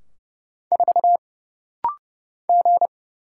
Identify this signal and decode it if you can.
Morse code, heard as a keyed tone, about 30 wpm, 700 Hz. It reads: 4 G
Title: Single Letter-Number Alt - Rapid-Fire 30wpm